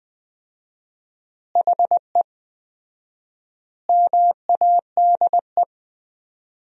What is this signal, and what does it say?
Morse code, heard as a keyed tone, about 21 wpm, 700 Hz. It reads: HE MADE
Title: 4x Word-Spacing - Sentences from Top 300 Words - 20wpm